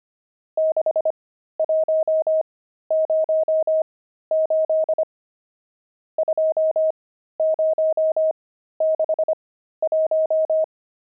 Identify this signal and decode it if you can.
Morse code, heard as a keyed tone, about 25 wpm, spaced as for 13 wpm, 630 Hz. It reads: 6108 2061
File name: pangram-fs-13-cs-25-nums3